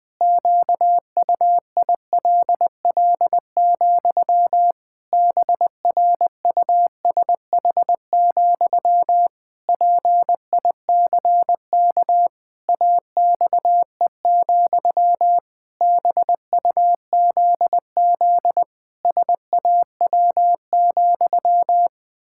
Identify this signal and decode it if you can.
Morse code, heard as a keyed tone, about 21 wpm, 700 Hz. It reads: QUILL, BRUSH, PICK AXE, BUZZ SAW,
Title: Pangram Challenge v2 - 20wpm